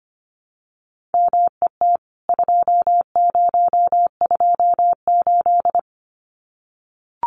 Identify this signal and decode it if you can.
Morse code, heard as a keyed tone, about 25 wpm, 700 Hz. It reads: MET 2028
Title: CWT Exchange 25wpm